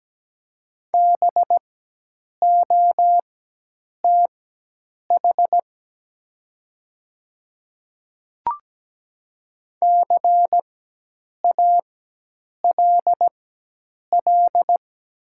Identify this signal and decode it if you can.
Morse code, heard as a keyed tone, about 17 wpm, spaced as for 8 wpm, 700 Hz. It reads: BOTH CALL
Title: 4x Character Spacing - Top 300 Words - 17wpm